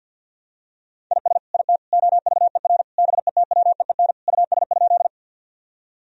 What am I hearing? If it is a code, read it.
Morse code, heard as a keyed tone, about 47 wpm, 700 Hz. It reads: IS IT OVER BETWEEN US?
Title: Sentences from Top 1000 Words - Review Part 1 - 50wpm